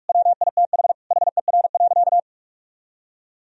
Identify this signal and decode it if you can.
Morse code, heard as a keyed tone, about 46 wpm, 700 Hz. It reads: WITH HER.
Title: Sentences from Top 1000 Words - Review Part 2 - 45wpm